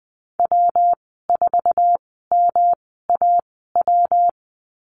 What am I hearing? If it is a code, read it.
Morse code, heard as a keyed tone, about 20 wpm, spaced as for 14 wpm, 700 Hz. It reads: W4MAW
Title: 140 Lesson 23g - Letter f - Call Signs 20wpm